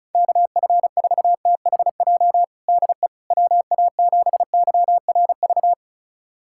Morse code, heard as a keyed tone, about 36 wpm, 700 Hz. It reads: KF4THJ DE WA7YRV
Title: USA Callsign DE Callsign 35wpm